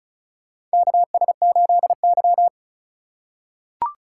Morse code, heard as a keyed tone, about 36 wpm, 700 Hz. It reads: KS8Y
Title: USA Call Signs - 4 Characters - 2x Character Spacing - 35wpm